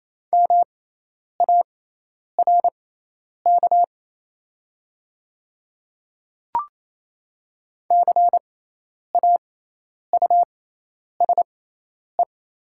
Morse code, heard as a keyed tone, about 28 wpm, spaced as for 10 wpm, 700 Hz. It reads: MARK CAUSE
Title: 6x Character Spacing - Top 300 Words - 28wpm